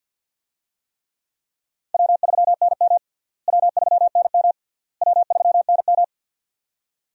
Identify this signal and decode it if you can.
Morse code, heard as a keyed tone, about 46 wpm, 700 Hz. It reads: W3NM W3NM W3NM
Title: USA Call Signs - 4 Characters - Repeated 3x - 50wpm